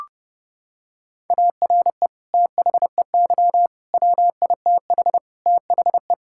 Morse code, heard as a keyed tone, about 32 wpm, 700 Hz. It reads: ARE THEY WITH THE
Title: Sentences from Top 1000 Words - Review Part 5 - 30wpm